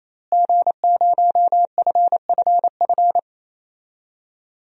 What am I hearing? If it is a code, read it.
Morse code, heard as a keyed tone, about 28 wpm, 700 Hz. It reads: G0FFF
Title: DXCC-100 Call Signs - 5 Characters - Repeated 3x - 28wpm